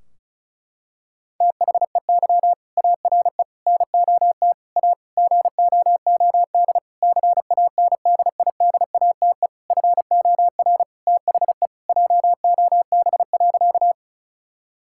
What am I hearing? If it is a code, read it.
Morse code, heard as a keyed tone, about 36 wpm, 700 Hz. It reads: THEY ARE NOT A GOOD CANDIDATE FOR THE JOB.
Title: Sentences from Top 1600 Words 35wpm